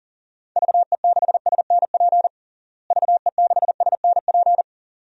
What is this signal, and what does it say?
Morse code, heard as a keyed tone, about 40 wpm, 700 Hz. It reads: VE6SNP VE6SNP
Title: DXCC-100 Call Signs - 6 Characters - Repeated 2x - 40wpm